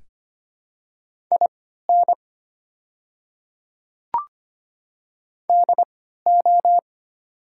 Morse code, heard as a keyed tone, about 25 wpm, spaced as for 14 wpm, 700 Hz. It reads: IN DO
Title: 3x Character Spacing - Top 500 Words - 25wpm